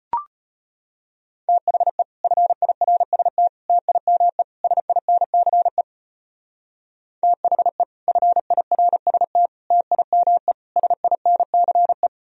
Morse code, heard as a keyed tone, about 31 wpm, 700 Hz. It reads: THE FIRST TIME SINCE THE FIRST TIME SINCE
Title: Speed-Racing - Sets of 4 Words 25wpm